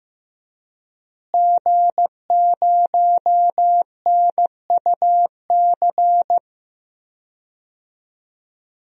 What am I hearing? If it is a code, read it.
Morse code, heard as a keyed tone, about 15 wpm, 700 Hz. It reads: G0NUC
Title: DXCC-100 Call Signs - 5 Characters - Repeated 2x - 15wpm